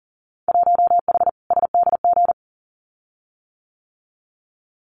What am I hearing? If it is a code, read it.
Morse code, heard as a keyed tone, about 42 wpm, 700 Hz. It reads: 1H SDG
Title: Winter Field Day Exchanges 40wpm